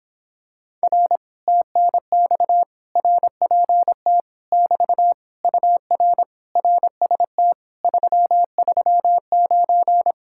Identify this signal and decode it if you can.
Morse code, heard as a keyed tone, about 26 wpm, 700 Hz. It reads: R TNX RPT <BT> UR RST 339